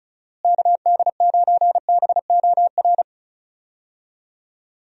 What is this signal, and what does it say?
Morse code, heard as a keyed tone, about 36 wpm, 700 Hz. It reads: KD9BOR
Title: USA Call Signs - 6 Characters - Repeated 3x - 35wpm